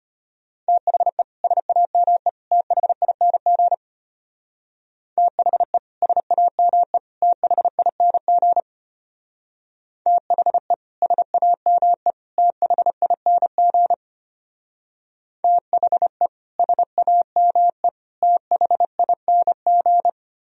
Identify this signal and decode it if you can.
Morse code, heard as a keyed tone, about 30 wpm, 700 Hz. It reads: THE SAME THING THE SAME THING THE SAME THING THE SAME THING
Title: Speed-Racing - Sets of 3 Words 25wpm